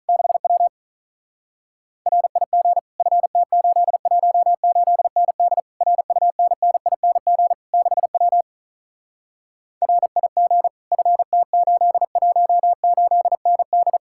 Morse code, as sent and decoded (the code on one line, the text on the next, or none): -.... .-- / .-. .. --. / ..-. - ---.. .---- ---.. -. -.. / .-. ..- -. -. .. -. --. / -.... .-- / .-. .. --. / ..-. - ---.. .---- ---.. -. -..
6W RIG FT818ND RUNNING 6W RIG FT818ND